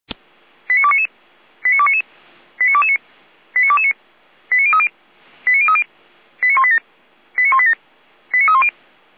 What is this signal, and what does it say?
Five-tone selective call (ZVEI), heard as quick runs of five tones, tones of about 70 ms. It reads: 89299 89299 89229 89229 89030 89030 89187 89187 89310